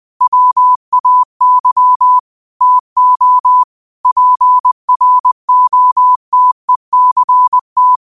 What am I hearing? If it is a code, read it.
Morse code, heard as a keyed tone, about 20 wpm, 1000 Hz. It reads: WAY TO PROTECT
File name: qotdfiles / 20wpm